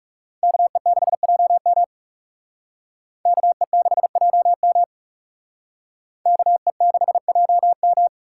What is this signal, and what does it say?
Morse code, heard as a keyed tone, about 37 wpm, 700 Hz. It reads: KE6JM KE6JM KE6JM
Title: Speed Racing - USA Call Signs 30wpm